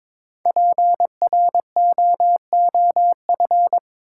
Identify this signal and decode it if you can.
Morse code, heard as a keyed tone, about 22 wpm, 700 Hz. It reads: PROOF
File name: 5 Letter Words - First 3 Letters Given - 22wpm